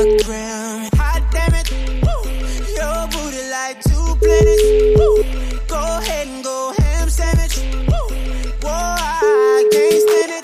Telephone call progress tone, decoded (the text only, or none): ringback tone